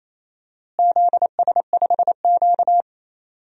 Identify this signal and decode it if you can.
Morse code, heard as a keyed tone, about 27 wpm, 700 Hz. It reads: ZS5Q
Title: DXCC-100 Call Signs - 4 Characters - Repeated 2x - 28wpm